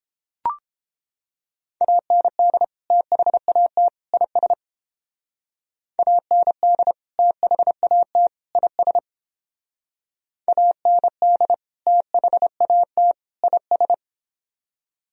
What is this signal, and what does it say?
Morse code, heard as a keyed tone, about 31 wpm, 700 Hz. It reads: AND THAT IS AND THAT IS AND THAT IS